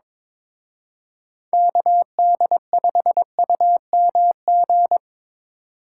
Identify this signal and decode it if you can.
Morse code, heard as a keyed tone, about 26 wpm, 700 Hz. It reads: KD5UMG
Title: Speed Racing - USA Call Signs 22wpm